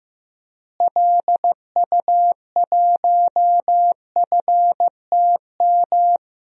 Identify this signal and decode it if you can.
Morse code, heard as a keyed tone, about 15 wpm, 700 Hz. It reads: LU1FTM
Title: DXCC-100 Call Signs - 6 Characters - Repeated 3x - 15wpm